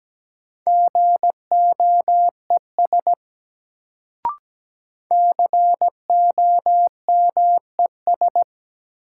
Morse code, heard as a keyed tone, about 17 wpm, 700 Hz. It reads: GOES COMES
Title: Sentences from Top 1000 Words - Review Part 5 - New Word Drill - 17wpm